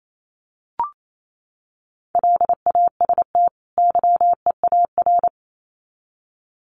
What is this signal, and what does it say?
Morse code, heard as a keyed tone, about 28 wpm, 700 Hz. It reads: LAST YEAR
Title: Sets of 2 Words 28wpm